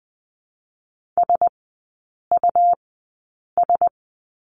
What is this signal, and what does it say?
Morse code, heard as a keyed tone, about 20 wpm, spaced as for 8 wpm, 700 Hz. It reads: SUS
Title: 166 Lesson 27e - Number 7 - Three Characters 20wpm